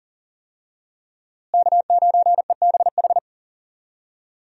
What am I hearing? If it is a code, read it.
Morse code, heard as a keyed tone, about 40 wpm, 700 Hz. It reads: K9EBH
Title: USA Call Signs - 5 Characters - Repeated 2x - 40wpm